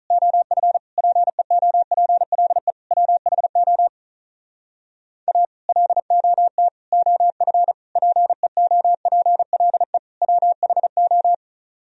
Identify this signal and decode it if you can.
Morse code, heard as a keyed tone, about 46 wpm, 700 Hz. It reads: OF PEOPLE WHO A LOT OF PEOPLE WHO